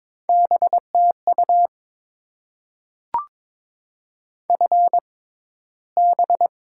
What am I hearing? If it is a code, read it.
Morse code, heard as a keyed tone, about 22 wpm, spaced as for 14 wpm, 700 Hz. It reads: BTU FB
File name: Top 20 QSO Elements - 6x Character Spacing - 22wpm